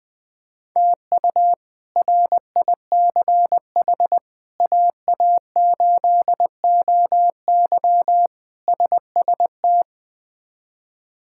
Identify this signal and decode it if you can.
Morse code, heard as a keyed tone, about 21 wpm, 700 Hz. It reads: TU RICH AA8OY SST
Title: K1USN SST QSO Practice - 20wpm